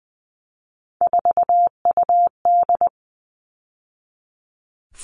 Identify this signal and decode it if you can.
Morse code, heard as a keyed tone, about 20 wpm, 700 Hz. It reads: 4UD
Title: All Cut Numbers - 3-Digit 20wpm